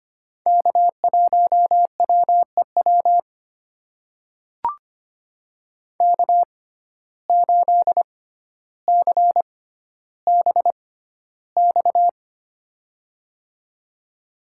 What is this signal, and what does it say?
Morse code, heard as a keyed tone, about 25 wpm, spaced as for 13 wpm, 700 Hz. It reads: K1WEW K8CBX
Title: USA Call Signs - 5 Characters - 6x Character Spacing - 25wpm